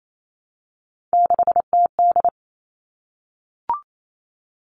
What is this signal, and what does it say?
Morse code, heard as a keyed tone, about 28 wpm, 700 Hz. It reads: 6TD
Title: All Cut Numbers - 3-Digit 28wpm